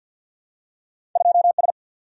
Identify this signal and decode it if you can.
Morse code, heard as a keyed tone, about 50 wpm, 700 Hz. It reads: 2 S
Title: Single-Letter Single-Number 50wpm